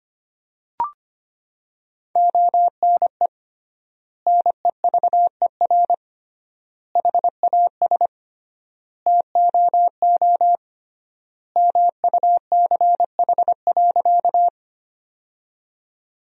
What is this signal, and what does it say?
Morse code, heard as a keyed tone, about 25 wpm, 700 Hz. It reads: ONE NEVER HAS TOO MUCH.
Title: 3x Word-Spacing - Sentences from Top 300 Words - 25wpm